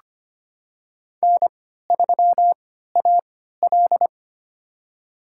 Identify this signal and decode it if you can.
Morse code, heard as a keyed tone, about 25 wpm, spaced as for 14 wpm, 700 Hz. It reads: N3AL
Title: USA Call Signs - 3x Character Spacing - 25wpm